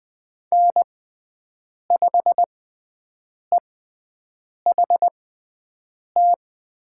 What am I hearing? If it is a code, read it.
Morse code, heard as a keyed tone, about 20 wpm, spaced as for 7 wpm, 700 Hz. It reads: N5EHT
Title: USA Call Signs - 5 Characters - 6x Character Spacing - 20wpm